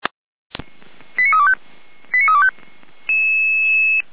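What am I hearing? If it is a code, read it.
Five-tone selective call (ZVEI), heard as quick runs of five tones, tones of about 70 ms. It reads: 89325 89325
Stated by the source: SDR receiver recording of ZVEI five-tone selective call 88074.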